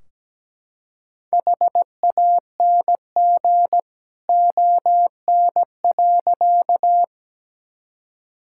Morse code, heard as a keyed tone, about 17 wpm, 700 Hz. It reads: HANG ON.
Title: Sentences from Top 1600 Words 17wpm